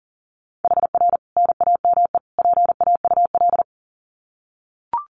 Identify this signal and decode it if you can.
Morse code, heard as a keyed tone, about 40 wpm, 700 Hz. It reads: HR NAME PAUL
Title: Common QSO Phrases 40wpm